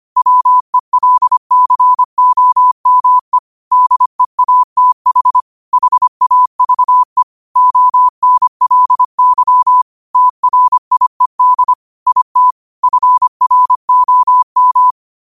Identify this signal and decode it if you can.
Morse code, heard as a keyed tone, about 25 wpm, 1000 Hz. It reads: WELCOME DEATH HAVE ONLY TRIED IT FROM
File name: qotdfiles / 25wpm